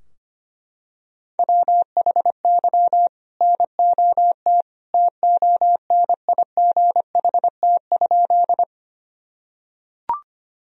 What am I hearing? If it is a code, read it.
Morse code, heard as a keyed tone, about 25 wpm, 700 Hz. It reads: WHY NOT TONIGHT?
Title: Sentences from Top 1000 Words - Review Part 3 - 25wpm